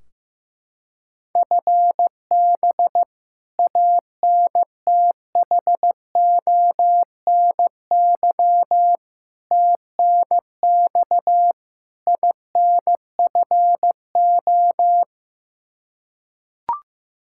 Morse code, heard as a keyed tone, about 15 wpm, 700 Hz. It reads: FB ANTHONY TNX INFO
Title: Practice QSO - Exchange 3 - More Info - 15wpm